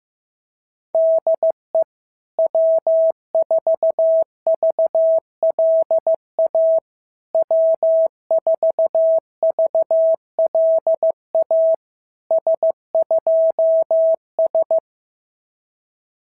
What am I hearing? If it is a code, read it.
Morse code, heard as a keyed tone, about 14 wpm, 650 Hz. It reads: DE W4VLA W4VLA S2S
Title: SOTA QSO Practice 15wpm